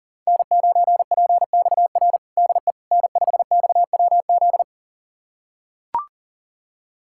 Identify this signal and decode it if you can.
Morse code, heard as a keyed tone, about 37 wpm, 700 Hz. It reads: N9PXR DE N5XWZ
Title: USA Callsign DE Callsign - 5 Characters - 40wpm